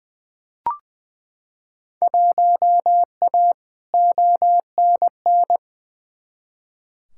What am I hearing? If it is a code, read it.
Morse code, heard as a keyed tone, about 20 wpm, 700 Hz. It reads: 1A ONN
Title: Field Day Exchanges Encore - 20wpm